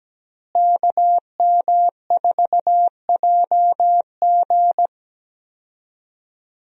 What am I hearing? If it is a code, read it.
Morse code, heard as a keyed tone, about 17 wpm, 700 Hz. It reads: KM4JG
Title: USA Call Signs - 5 Characters - Repeated 2x - 17wpm